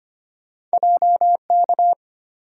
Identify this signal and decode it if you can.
Morse code, heard as a keyed tone, about 25 wpm, 700 Hz. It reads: JK
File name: Two Letter-Number Combinations - 25wpm